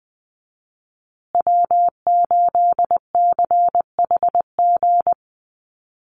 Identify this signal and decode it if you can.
Morse code, heard as a keyed tone, about 20 wpm, 700 Hz. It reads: W8CHG
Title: USA Call Signs Repeated 3x 20wpm